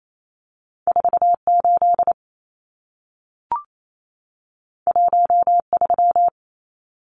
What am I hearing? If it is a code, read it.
Morse code, heard as a keyed tone, about 28 wpm, 700 Hz. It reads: 48 13
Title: Numbers - 2-Digit 28wpm